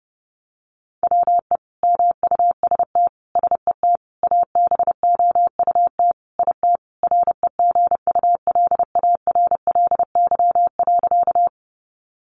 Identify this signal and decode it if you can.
Morse code, heard as a keyed tone, about 30 wpm, 700 Hz. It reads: WE MUST SET ABOUT IT REGULARLY.
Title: Sentences from Top 1400 Words 30wpm